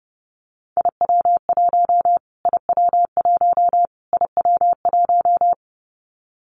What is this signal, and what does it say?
Morse code, heard as a keyed tone, about 30 wpm, 700 Hz. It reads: IW1 IW1 IW1
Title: Callsign Prefixes for Top 25 DXCC - Repeated 3x - 30wpm